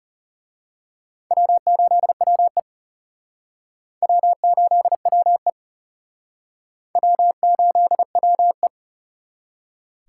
Morse code, heard as a keyed tone, about 34 wpm, 700 Hz. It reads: W8WE W8WE W8WE
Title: Speed Racing - USA Call Signs 30wpm